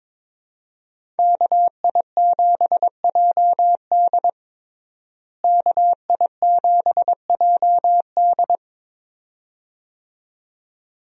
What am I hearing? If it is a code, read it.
Morse code, heard as a keyed tone, about 22 wpm, 700 Hz. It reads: KI7JD KI7JD
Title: USA Call Signs - 5 Characters - Repeated 2x - 22wpm